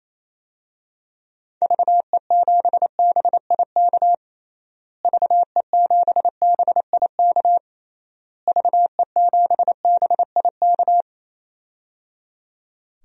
Morse code, heard as a keyed tone, about 27 wpm, 700 Hz. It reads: VE7BIK VE7BIK VE7BIK
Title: DXCC-100 Call Signs - 6 Characters - Repeated 3x - 28wpm